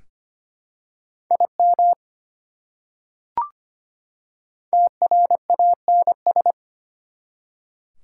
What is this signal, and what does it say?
Morse code, heard as a keyed tone, about 25 wpm, 700 Hz. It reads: IM TRANS
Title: Prefixes and Suffixes 25wpm